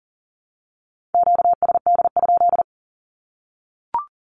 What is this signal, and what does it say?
Morse code, heard as a keyed tone, about 40 wpm, 700 Hz. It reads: QSD?
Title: Q-Codes 40wpm